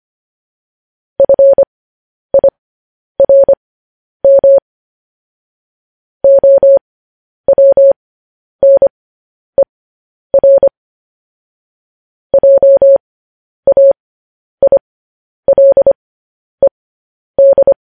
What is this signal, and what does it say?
Morse code, heard as a keyed tone, about 25 wpm, spaced as for 10 wpm, 550 Hz. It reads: FIRM OWNER JAILED
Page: news headlines in morse code at 10 wpm